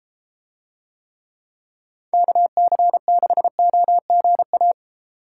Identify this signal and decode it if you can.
Morse code, heard as a keyed tone, about 33 wpm, 700 Hz. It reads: KC6OGA